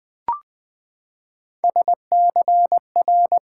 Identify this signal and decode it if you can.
Morse code, heard as a keyed tone, about 20 wpm, 700 Hz. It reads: SCR
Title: Build A Word - 3 Letters At A Time - 6-Letter Words - 20wpm